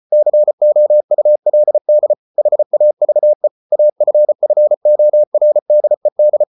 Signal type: Morse code